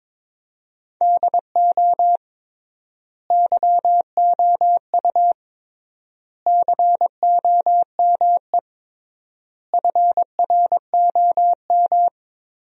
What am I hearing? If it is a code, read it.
Morse code, heard as a keyed tone, about 22 wpm, 700 Hz. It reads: DO YOU COME FROM
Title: 3x Word-Spacing - Sentences from Top 200 Words - 22wpm